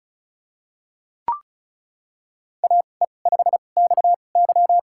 Morse code, heard as a keyed tone, about 36 wpm, spaced as for 25 wpm, 700 Hz. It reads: AE5XY
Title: USA Call Signs - 5 Characters - 2x Character Spacing - 35wpm